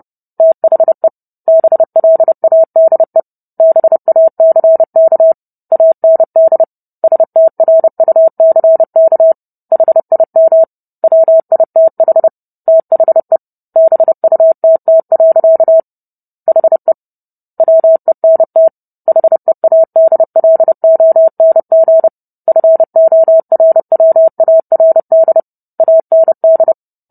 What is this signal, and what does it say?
Morse code, heard as a keyed tone, about 30 wpm, 650 Hz. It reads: THE BLADE BACK AND STRUCK HIM WITH THE BUTT. HE WENT HEADLONG FORWARD AND